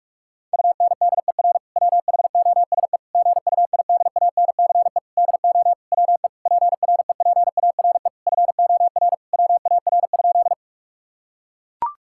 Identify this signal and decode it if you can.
Morse code, heard as a keyed tone, about 41 wpm, 700 Hz. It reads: UNDER WHOSE GUIDANCE DO WE PREPARE FOR WAR?
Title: Sentences from Top 1600 Words 45wpm